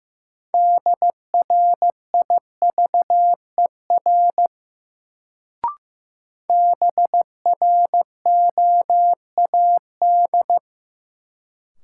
Morse code, heard as a keyed tone, about 15 wpm, 700 Hz. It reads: DRIVER BROAD
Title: Sentences from Top 1000 Words - New Word Drill - 15wpm